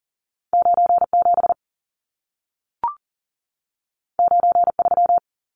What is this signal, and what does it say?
Morse code, heard as a keyed tone, about 40 wpm, 700 Hz. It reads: 97 93
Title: Numbers - 2-Digit 40wpm